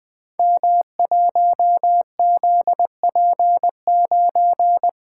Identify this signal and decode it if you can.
Morse code, heard as a keyed tone, about 20 wpm, 700 Hz. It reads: M1ZP9
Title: Code Groups - 5 Letter:Numbers - 20wpm